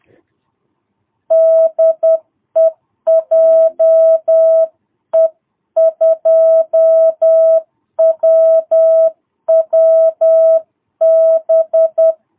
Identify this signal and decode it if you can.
Morse code, heard as a keyed tone, about 10 wpm, 650 Hz. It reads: DE JE2WWB
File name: jh8cbh / 2022cw